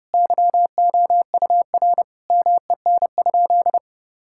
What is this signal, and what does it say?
Morse code, heard as a keyed tone, about 30 wpm, 700 Hz. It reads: YOUR MEN?
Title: Sentences from Top 1000 Words - Review Part 6 - 30wpm